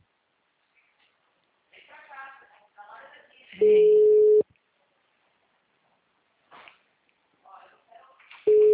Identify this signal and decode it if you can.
Ringback tone (Europe, 425 Hz), with speech.